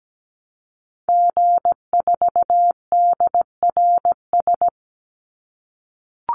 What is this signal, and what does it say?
Morse code, heard as a keyed tone, about 17 wpm, 700 Hz. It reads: G4DRS
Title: Call Signs - 17wpm